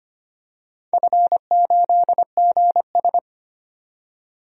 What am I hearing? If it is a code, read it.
Morse code, heard as a keyed tone, about 25 wpm, 700 Hz. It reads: F8GS
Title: DXCC-100 Call Signs - 4 Characters - 25wpm